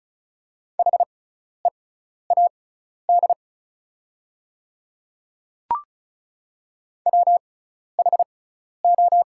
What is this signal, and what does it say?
Morse code, heard as a keyed tone, about 36 wpm, spaced as for 12 wpm, 690 Hz. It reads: HEAD WHO
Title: 6x Character Spacing - Top 300 Words - 35wpm